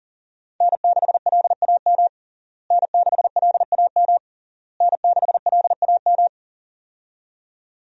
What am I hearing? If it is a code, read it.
Morse code, heard as a keyed tone, about 40 wpm, 700 Hz. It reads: N6LAM N6LAM N6LAM
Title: USA Call Signs - 5 Characters - Repeated 3x - 40wpm